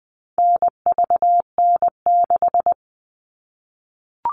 Morse code, heard as a keyed tone, about 20 wpm, 700 Hz. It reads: NVN6
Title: All Cut Numbers - 4-Digit 20wpm